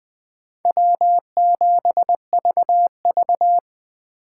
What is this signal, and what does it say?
Morse code, heard as a keyed tone, about 24 wpm, 700 Hz. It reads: W7VV
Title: Speed Racing - USA 1x2 Call Signs 20wpm